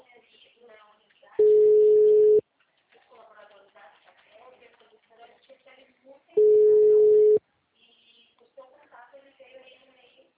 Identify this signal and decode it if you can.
Ringback tone (Europe, 425 Hz), with speech.